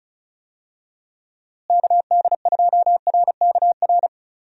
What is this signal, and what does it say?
Morse code, heard as a keyed tone, about 34 wpm, 700 Hz. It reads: KD2RKR
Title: USA Call Signs - 6 Characters - Repeated 2x - 35wpm